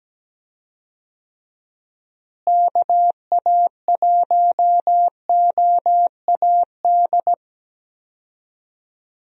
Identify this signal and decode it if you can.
Morse code, heard as a keyed tone, about 17 wpm, 700 Hz. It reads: KA1OAD
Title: USA Call Signs - 6 Characters - Repeated 3x - 17wpm